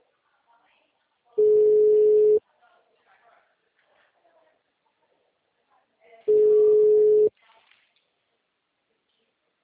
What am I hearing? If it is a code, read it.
Ringback tone (Europe, 425 Hz), with speech.
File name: out- -2019- - - .26826